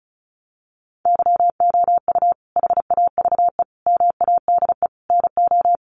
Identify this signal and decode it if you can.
Morse code, heard as a keyed tone, about 34 wpm, 700 Hz. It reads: YOU HAVE MADE NO
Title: Sentences from Top 1400 Words 35wpm